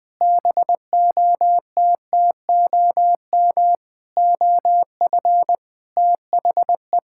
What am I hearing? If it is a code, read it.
Morse code, heard as a keyed tone, about 20 wpm, 700 Hz. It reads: BOTTOM OF THE
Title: Sentences with 5-Word Phrases - Spoken Beginning - 20wpm